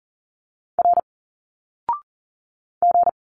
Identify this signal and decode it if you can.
Morse code, heard as a keyed tone, about 40 wpm, 700 Hz. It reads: R G
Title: Single Letters 40wpm